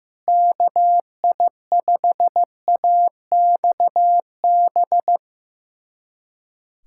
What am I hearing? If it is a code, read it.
Morse code, heard as a keyed tone, about 15 wpm, 700 Hz. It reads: KI5AXB
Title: USA Callsign DE Callsign 15wpm